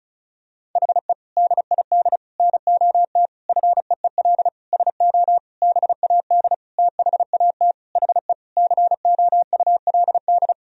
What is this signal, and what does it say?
Morse code, heard as a keyed tone, about 34 wpm, 700 Hz. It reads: HE DID NOT FEEL SO BAD THAT HE COULD
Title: Sentences from Top 1000 Words - Review Part 4 - 35wpm